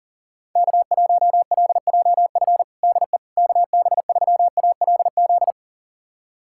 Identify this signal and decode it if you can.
Morse code, heard as a keyed tone, about 40 wpm, 700 Hz. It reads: K1LJF DE KB3ALZ
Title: USA Callsign DE Callsign 40wpm